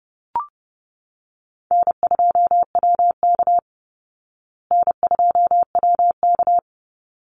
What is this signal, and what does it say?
Morse code, heard as a keyed tone, about 30 wpm, 700 Hz. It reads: N2WK N2WK
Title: Easy Call Signs Repeated 2x - 30wpm